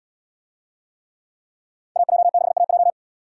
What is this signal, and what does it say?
Morse code, heard as a keyed tone, about 77 wpm, 700 Hz. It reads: AJ6AJ